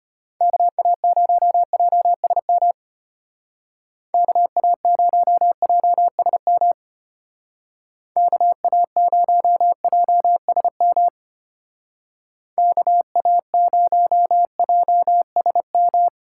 Morse code, heard as a keyed tone, about 30 wpm, 700 Hz. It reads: KA0JSM KA0JSM KA0JSM KA0JSM
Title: Speed Racing - USA 2x3 Call Signs 25wpm